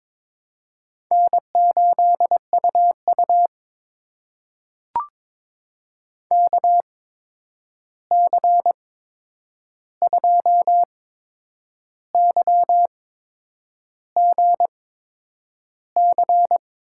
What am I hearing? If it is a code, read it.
Morse code, heard as a keyed tone, about 22 wpm, spaced as for 8 wpm, 700 Hz. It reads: N8UU KC2YGC
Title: USA Call Signs - 8x Character Spacing - 22wpm